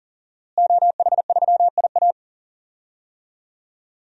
Morse code, heard as a keyed tone, about 40 wpm, 700 Hz. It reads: OH3IA
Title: DXCC-100 Call Signs - 5 Characters - Repeated 3x - 40wpm